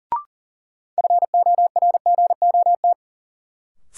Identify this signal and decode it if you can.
Morse code, heard as a keyed tone, about 40 wpm, 700 Hz. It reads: FORGOT
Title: Sentences from Top 1400 Words - New Word Drill Alt - 40wpm